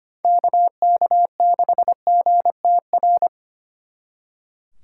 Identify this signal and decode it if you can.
Morse code, heard as a keyed tone, about 27 wpm, 700 Hz. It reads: KK6GTR
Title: Speed Racing - USA Call Signs 25wpm